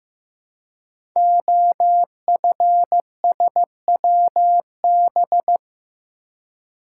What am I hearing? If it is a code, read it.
Morse code, heard as a keyed tone, about 15 wpm, 700 Hz. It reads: OFSWB
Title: Code Groups - 5 Letters - 15wpm